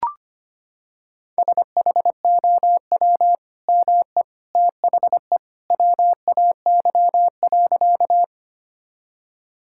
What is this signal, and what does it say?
Morse code, heard as a keyed tone, about 25 wpm, 700 Hz. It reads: SHOW ME THE WAY.
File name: practice / 25wpm / Sentences with 2-Word Phrases - 25wpm